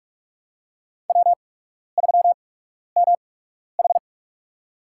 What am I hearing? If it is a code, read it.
Morse code, heard as a keyed tone, about 46 wpm, spaced as for 12 wpm, 700 Hz. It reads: W3MH
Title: USA Call Signs - 4 Characters - 8x Character Spacing - 45wpm